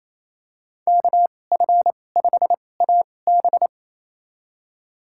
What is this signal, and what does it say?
Morse code, heard as a keyed tone, about 28 wpm, spaced as for 20 wpm, 700 Hz. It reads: KF5AB
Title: USA Call Signs - 2x Character Spacing - 28wpm